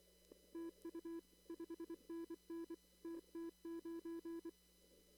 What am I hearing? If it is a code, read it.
Morse code, heard as a keyed tone, about 23 wpm, 330 Hz. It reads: TU 5NN TT9
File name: CW / kt5j wpx cw-op-k5pi